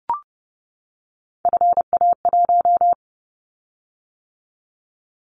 Morse code, heard as a keyed tone, about 30 wpm, 700 Hz. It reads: FA1
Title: Callsign Prefixes for Top 25 DXCC - 30wpm